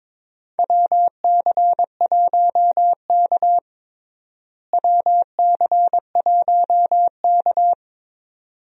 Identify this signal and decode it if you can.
Morse code, heard as a keyed tone, about 22 wpm, 700 Hz. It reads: WC1K WC1K
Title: USA Call Signs - 4 Characters - Repeated 2x - 22wpm